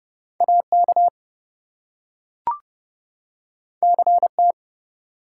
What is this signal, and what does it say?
Morse code, heard as a keyed tone, about 30 wpm, 700 Hz. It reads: AK CT
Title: US States and Canadian Provinces 30wpm